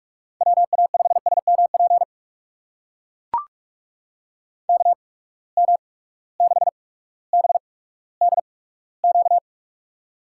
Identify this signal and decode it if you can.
Morse code, heard as a keyed tone, about 44 wpm, spaced as for 19 wpm, 700 Hz. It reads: WA5SMP KM6BDQ
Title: USA Call Signs - 6 Characters - 8x Character Spacing - 45wpm